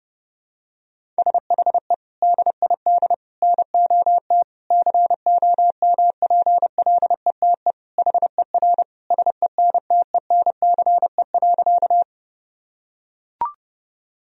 Morse code, heard as a keyed tone, about 32 wpm, 700 Hz. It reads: SHE DID NOT COMPLETE HER SENTENCE.
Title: Sentences from Top 1000 Words - Review Part 2 - 30wpm